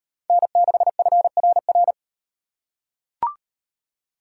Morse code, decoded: N6FRR

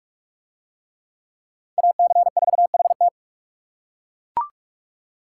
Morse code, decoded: AC4HT

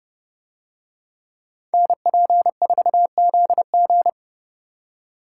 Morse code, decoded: NP4ZG

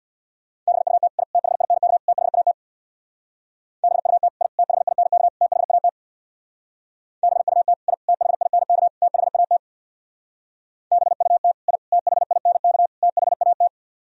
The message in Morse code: BUT I THINK THAT BUT I THINK THAT BUT I THINK THAT BUT I THINK THAT

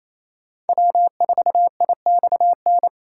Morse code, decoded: W4IXN

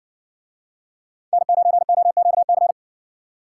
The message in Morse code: N9OQC

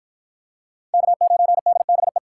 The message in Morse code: K9DBE

